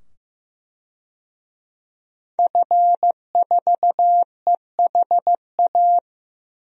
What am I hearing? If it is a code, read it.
Morse code, heard as a keyed tone, about 15 wpm, 700 Hz. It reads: F4EHA
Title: DXCC-100 Call Signs - 5 Characters - 15wpm